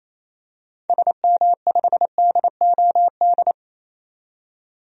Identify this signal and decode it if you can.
Morse code, heard as a keyed tone, about 28 wpm, 700 Hz. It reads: SM5DOD
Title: DXCC-100 Call Signs - 6 Characters - 28wpm